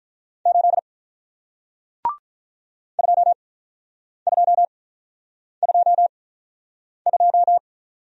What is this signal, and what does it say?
Morse code, heard as a keyed tone, about 46 wpm, spaced as for 9 wpm, 700 Hz. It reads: 8 2222